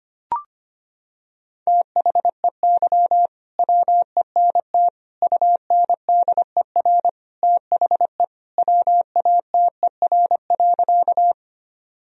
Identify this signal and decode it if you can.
Morse code, heard as a keyed tone, about 25 wpm, 700 Hz. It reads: THEY WENT UNDER THE WATER.
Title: Sentences from Top 1000 Words - Review Part 1 - 25wpm